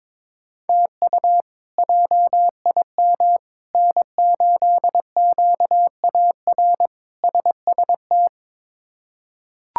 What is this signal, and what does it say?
Morse code, heard as a keyed tone, about 22 wpm, 700 Hz. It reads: TU JIM N8QAR SST
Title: K1USN SST QSO Practice - 22wpm